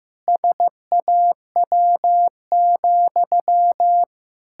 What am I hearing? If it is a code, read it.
Morse code, heard as a keyed tone, about 15 wpm, 700 Hz. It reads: SAW,
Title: Pangram Challenge v2 - 15wpm